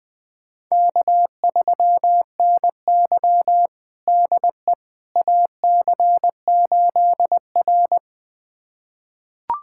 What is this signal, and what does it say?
Morse code, heard as a keyed tone, about 20 wpm, 700 Hz. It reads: K3NY DE AC8R
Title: USA Callsign DE Callsign - 4 Characters - 20wpm